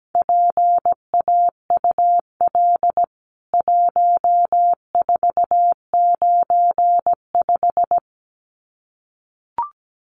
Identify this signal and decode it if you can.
Morse code, heard as a keyed tone, about 17 wpm, 700 Hz. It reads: PAUL 1495
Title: CWT Exchange 17wpm